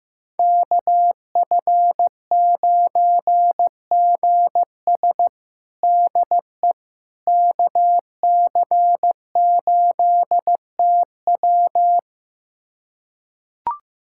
Morse code, heard as a keyed tone, about 15 wpm, 700 Hz. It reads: KF9GS DE KC8TW
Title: USA Callsign DE Callsign - 5 Characters - 15wpm